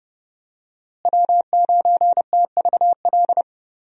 Morse code, heard as a keyed tone, about 30 wpm, 700 Hz. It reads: W9TVL
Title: USA Call Signs - 5 Characters - Repeated 2x - 30wpm